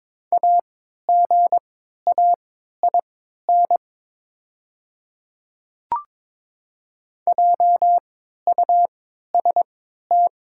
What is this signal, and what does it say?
Morse code, heard as a keyed tone, about 22 wpm, spaced as for 12 wpm, 700 Hz. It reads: AGAIN JUST
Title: 3x Character Spacing - Top 200 Words - 22wpm